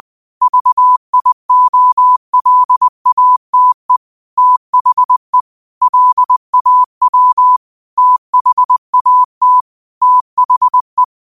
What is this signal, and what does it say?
Morse code, heard as a keyed tone, about 20 wpm, 1000 Hz. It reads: VIOLATE THE LAW THAT THE
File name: qotdfiles / 20wpm